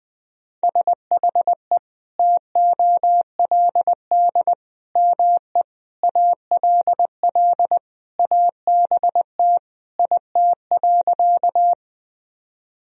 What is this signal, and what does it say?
Morse code, heard as a keyed tone, about 19 wpm, 700 Hz. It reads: SHE TOLD ME ALL ABT IT.
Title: Sentences from Top 400 Words and Abbreviations 20wpm